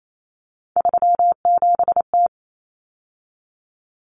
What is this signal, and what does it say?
Morse code, heard as a keyed tone, about 28 wpm, 700 Hz. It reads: 37T
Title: Common Cut Numbers - 3-Digit 28wpm